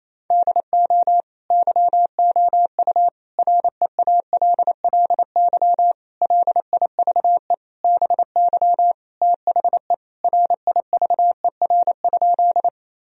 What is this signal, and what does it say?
Morse code, heard as a keyed tone, about 28 wpm, 700 Hz. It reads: DO YOU REALLY LIVE BY THE RIVER?
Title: Sentences from Top 1000 Words - Review Part 1 - 28wpm